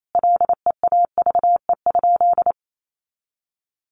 Morse code, heard as a keyed tone, about 28 wpm, 700 Hz. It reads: LEAVE?
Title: Sentences from Top 400 Words 28wpm